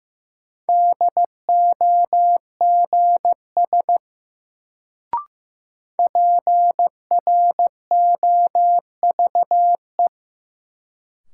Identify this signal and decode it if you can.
Morse code, heard as a keyed tone, about 16 wpm, 700 Hz. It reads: DOGS PROVE
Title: Sentences from Top 1600 Words - New Word Drill Alt - 15wpm